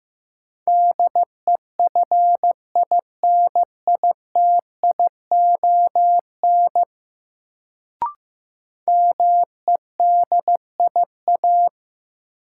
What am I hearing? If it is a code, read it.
Morse code, heard as a keyed tone, about 15 wpm, 700 Hz. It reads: DEFINITION MEDIA
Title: Sentences from Top 1000 Words - Review Part 4 - New Word Drill - 15wpm